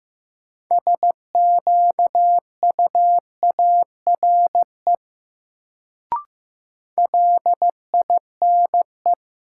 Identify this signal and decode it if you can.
Morse code, heard as a keyed tone, about 15 wpm, 700 Hz. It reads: SQUARE LINE
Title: Sentences from Top 1200 Words - New Word Drill Alt - 15wpm